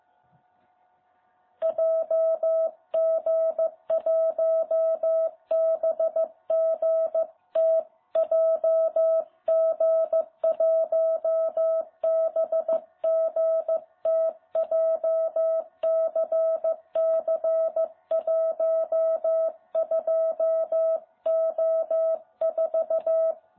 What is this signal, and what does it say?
Morse code, heard as a keyed tone, about 14 wpm, 660 Hz. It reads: JG1BGT JG1BGT JCC12O4
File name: jh8cbh / 2021cw